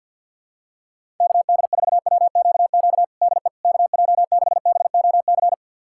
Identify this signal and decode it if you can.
Morse code, heard as a keyed tone, about 47 wpm, 700 Hz. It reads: KD4WQQ DE KJ6BOC
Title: USA Callsign DE Callsign - 6 Characters - 50wpm